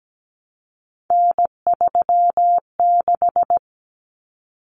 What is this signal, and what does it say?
Morse code, heard as a keyed tone, about 17 wpm, 700 Hz. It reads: N36
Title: Common Cut Numbers - 3-Digit 17wpm